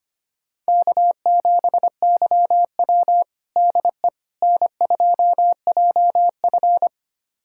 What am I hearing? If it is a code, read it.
Morse code, heard as a keyed tone, about 25 wpm, 700 Hz. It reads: K7YW DE N2JF
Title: USA Callsign DE Callsign - 4 Characters - 25wpm